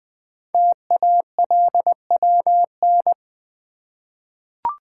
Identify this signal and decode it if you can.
Morse code, heard as a keyed tone, about 20 wpm, 700 Hz. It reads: TALWN
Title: Code Groups - 5 Letters - 20wpm